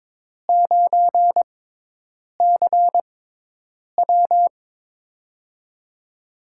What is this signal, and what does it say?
Morse code, heard as a keyed tone, about 21 wpm, spaced as for 8 wpm, 700 Hz. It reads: 9CW
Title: USA Call Signs - 5 Characters - 6x Character Spacing - 22wpm